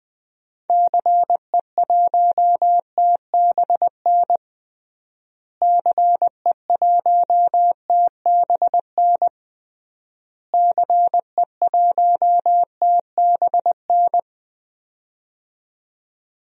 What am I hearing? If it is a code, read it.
Morse code, heard as a keyed tone, about 20 wpm, 700 Hz. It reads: CE1TBN CE1TBN CE1TBN
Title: DXCC-100 Call Signs - 6 Characters - Repeated 3x - 20wpm